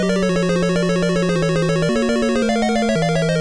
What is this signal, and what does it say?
There is no code.